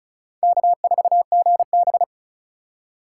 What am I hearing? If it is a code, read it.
Morse code, heard as a keyed tone, about 36 wpm, 700 Hz. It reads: K4GB
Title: USA Call Signs - 4x Character Spacing - 35wpm